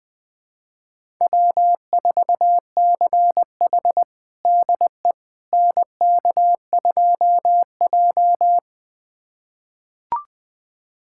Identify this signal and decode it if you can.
Morse code, heard as a keyed tone, about 20 wpm, 700 Hz. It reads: W4CH DE NK2J
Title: USA Callsign DE Callsign - 4 Characters - 20wpm